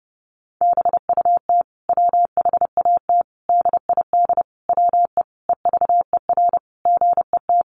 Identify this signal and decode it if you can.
Morse code, heard as a keyed tone, about 28 wpm, 700 Hz. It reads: BUT WHAT DID WE EVER GET
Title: Sentences from Top 400 Words 30wpm